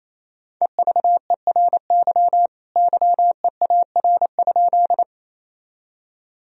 Morse code, heard as a keyed tone, about 28 wpm, 700 Hz. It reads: EVERY YEAR?
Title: Sentences from Top 1000 Words - Review Part 1 - 28wpm